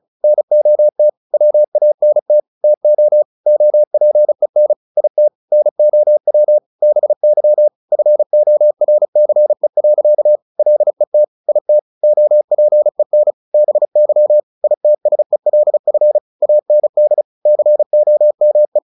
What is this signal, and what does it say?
Morse code, heard as a keyed tone, about 36 wpm, 600 Hz. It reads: NOT WANT TO OPEN IT NOW BY FORCE. LET IT OPEN BY ITSELF AND COME
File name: the old man and the sea / the old man and the sea 1 35wpm